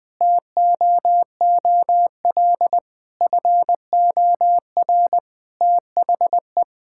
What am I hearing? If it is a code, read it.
Morse code, heard as a keyed tone, about 20 wpm, 700 Hz. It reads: TOOL FOR THE
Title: Sentences from Top 1000 Words - Review Part 6 - 20wpm